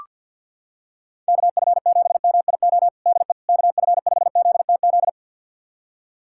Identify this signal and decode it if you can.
Morse code, heard as a keyed tone, about 47 wpm, 700 Hz. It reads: KF7MIO DE KF5ZTZ
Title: USA Callsign DE Callsign 50wpm